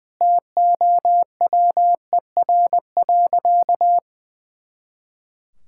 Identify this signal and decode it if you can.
Morse code, heard as a keyed tone, about 20 wpm, 700 Hz. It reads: TOWER.
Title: Pangram Challenge v2 - 20wpm